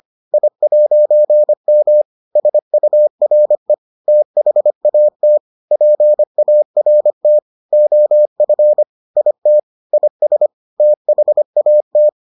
Morse code, heard as a keyed tone, about 25 wpm, 600 Hz. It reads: I'M SURE THAT PART OF IT IS THAT